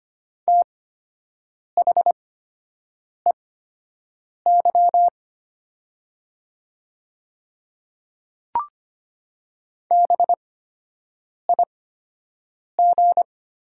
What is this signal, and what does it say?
Morse code, heard as a keyed tone, about 25 wpm, spaced as for 7 wpm, 700 Hz. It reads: THEY BIG
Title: 8x Character Spacing - Top 300 Words - 25wpm